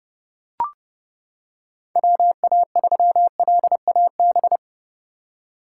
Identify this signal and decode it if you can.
Morse code, heard as a keyed tone, about 30 wpm, 700 Hz. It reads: WA3LAB
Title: K1USN SST QSO Practice - 30wpm